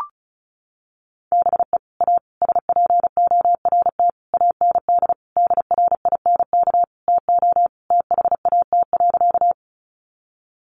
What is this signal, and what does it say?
Morse code, heard as a keyed tone, about 34 wpm, 700 Hz. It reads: BE A SPORT AND DRINK TO THAT.
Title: Sentences from Top 1500 Words 35wpm